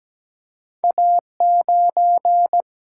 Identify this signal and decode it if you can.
Morse code, heard as a keyed tone, about 17 wpm, 700 Hz. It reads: A9
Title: Single-Letter Single-Number 17wpm